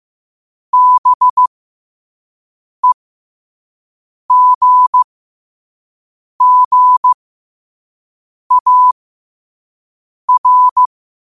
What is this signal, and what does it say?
Morse code, heard as a keyed tone, about 15 wpm, spaced as for 5 wpm, 1000 Hz. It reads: BEGGAR
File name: qotdfiles / 5wpm